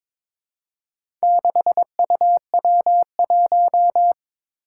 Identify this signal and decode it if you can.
Morse code, heard as a keyed tone, about 22 wpm, 700 Hz. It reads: 6UW1
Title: Four Challenging Letter-Number Combinations 22wpm